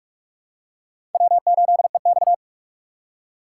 Morse code, heard as a keyed tone, about 44 wpm, 690 Hz. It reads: W8EX
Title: USA Call Signs - 4 Characters - 2x Character Spacing - 45wpm